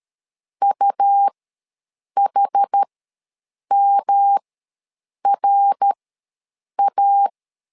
Morse code, heard as a keyed tone, about 13 wpm, spaced as for 7 wpm, 800 Hz. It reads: UHMRA
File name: LEZIONI / 3 MWRO / 3 MWRO ws 12wpm 1914